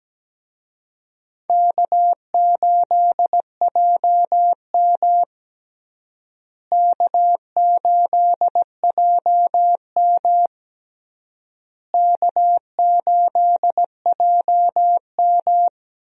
Morse code, heard as a keyed tone, about 17 wpm, 700 Hz. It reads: K8JM K8JM K8JM
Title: USA Call Signs - 4 Characters - Repeated 3x - 17wpm